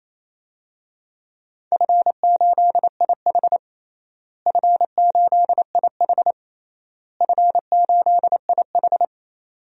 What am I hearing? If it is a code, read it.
Morse code, heard as a keyed tone, about 27 wpm, 700 Hz. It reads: F8IH F8IH F8IH
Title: DXCC-100 Call Signs - 4 Characters - Repeated 3x - 28wpm